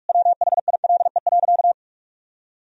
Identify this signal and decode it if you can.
Morse code, heard as a keyed tone, about 46 wpm, 700 Hz. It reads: WHILE.
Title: Sentences from Top 1000 Words - Review Part 2 - 45wpm